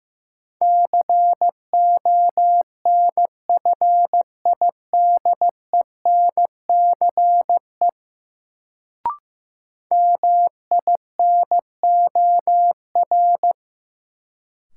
Morse code, heard as a keyed tone, about 15 wpm, 700 Hz. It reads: CONFIDENCE MINOR
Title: Sentences from Top 1500 Words - New Word Drill Alt - 15wpm